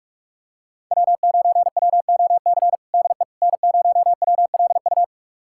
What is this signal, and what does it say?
Morse code, heard as a keyed tone, about 46 wpm, 700 Hz. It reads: W9WOC DE N0WLU
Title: USA Callsign DE Callsign - 5 Characters - 45wpm